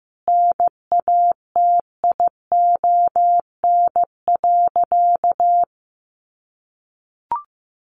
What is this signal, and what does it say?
Morse code, heard as a keyed tone, about 15 wpm, 700 Hz. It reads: NATION.
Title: Sentences from Top 1400 Words 15wpm